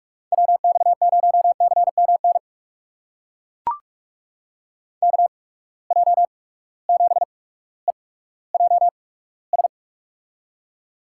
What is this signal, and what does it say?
Morse code, heard as a keyed tone, about 42 wpm, spaced as for 19 wpm, 700 Hz. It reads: WX0CMN KJ7EJS
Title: USA Call Signs - 6 Characters - 8x Character Spacing - 45wpm